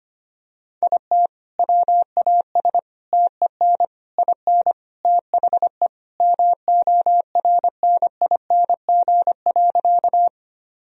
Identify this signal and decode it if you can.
Morse code, heard as a keyed tone, about 25 wpm, 700 Hz. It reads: IT WAS TEN IN THE MORNING.
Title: Sentences from Top 1000 Words - Review Part 2 - 25wpm